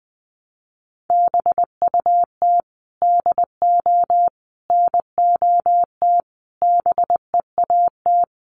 Morse code, heard as a keyed tone, about 20 wpm, 700 Hz. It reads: BUT DO NOT BEAT
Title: Sentences from Top 1300 Words 20wpm